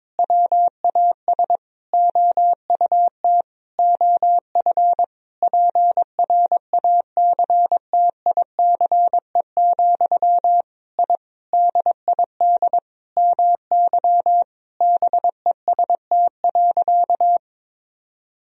Morse code, heard as a keyed tone, about 21 wpm, 700 Hz. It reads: WAS OUT OF PRACTICE, I DID MY BEST.
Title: Sentences from Top 1000 Words - Review Part 4 - 22wpm